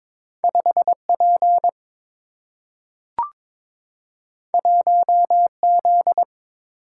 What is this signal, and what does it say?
Morse code, heard as a keyed tone, about 22 wpm, 700 Hz. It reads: 5 P 1 Z
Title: Single-Letter Single-Number 22wpm